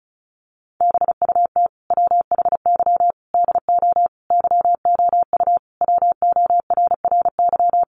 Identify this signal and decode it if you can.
Morse code, heard as a keyed tone, about 34 wpm, 700 Hz. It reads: BUT WHY DO YOU WORRY
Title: Sentences from Top 1100 Words 35wpm